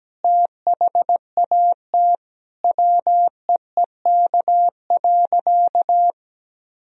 Morse code, heard as a keyed tone, about 17 wpm, 700 Hz. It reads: THAT WEEK.
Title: Sentences from Top 1000 Words - Review Part 2 - 17wpm